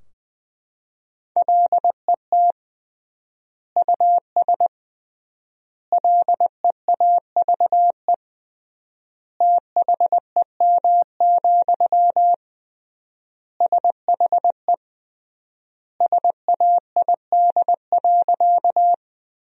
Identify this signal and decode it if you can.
Morse code, heard as a keyed tone, about 20 wpm, 700 Hz. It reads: LET US LEAVE THEM, SHE SAID.
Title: 3x Word-Spacing - Sentences from Top 400 Words - 20wpm